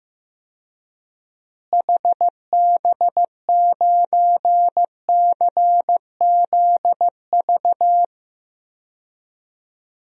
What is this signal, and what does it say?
Morse code, heard as a keyed tone, about 15 wpm, 700 Hz. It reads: HB9CZV
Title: DXCC-100 Call Signs - 6 Characters - Repeated 3x - 15wpm